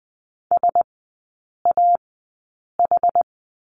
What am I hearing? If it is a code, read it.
Morse code, heard as a keyed tone, about 20 wpm, spaced as for 8 wpm, 700 Hz. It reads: SAH
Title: 074 Lesson 13e - Letter L - Three Characters 20wpm